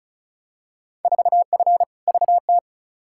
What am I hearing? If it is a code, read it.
Morse code, heard as a keyed tone, about 36 wpm, 700 Hz. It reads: 4F VT
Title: Field Day Exchanges Encore - 35wpm